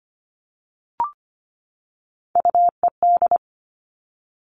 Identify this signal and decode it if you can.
Morse code, heard as a keyed tone, about 25 wpm, 700 Hz. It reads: UED
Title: Three Challenging Letter-Number Combinations 25wpm